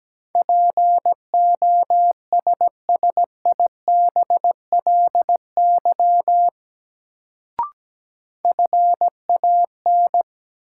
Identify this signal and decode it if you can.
Morse code, heard as a keyed tone, about 17 wpm, 700 Hz. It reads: POSSIBLY FAN
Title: Sentences from Top 1300 Words - New Word Drill Alt - 17wpm